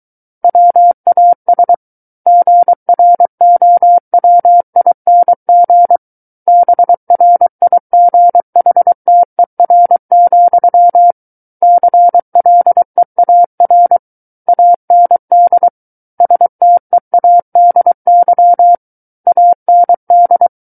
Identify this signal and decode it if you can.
Morse code, heard as a keyed tone, about 23 wpm, 700 Hz. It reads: WAS GROWING BRIGHTER, CLEAR AND STEADY AND
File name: Book One / 14-Chapter 14 - 23 WPM